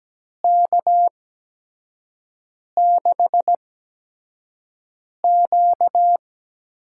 Morse code, heard as a keyed tone, about 17 wpm, spaced as for 5 wpm, 700 Hz. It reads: K6Q
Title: USA Call Signs - 8x Character Spacing - 17wpm